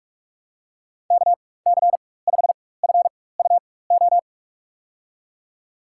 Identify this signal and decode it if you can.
Morse code, heard as a keyed tone, about 46 wpm, spaced as for 21 wpm, 700 Hz. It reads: KC5FUO
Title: USA Call Signs - 6 Characters - 4x Character Spacing - 45wpm